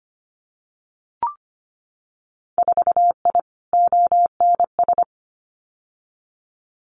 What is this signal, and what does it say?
Morse code, heard as a keyed tone, about 25 wpm, 700 Hz. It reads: 4I ONS
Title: Winter Field Day Exchanges Encore 25wpm